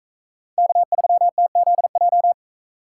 Morse code, heard as a keyed tone, about 40 wpm, 700 Hz. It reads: K3TZJ